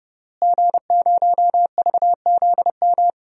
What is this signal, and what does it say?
Morse code, heard as a keyed tone, about 30 wpm, 700 Hz. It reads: G0VZM
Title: DXCC-100 Call Signs - 5 Characters - Repeated 3x - 30wpm